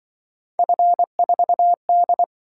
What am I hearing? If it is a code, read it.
Morse code, heard as a keyed tone, about 24 wpm, 700 Hz. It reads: F4D
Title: Speed-Racing - Three Letter-Number Combinations - 20wpm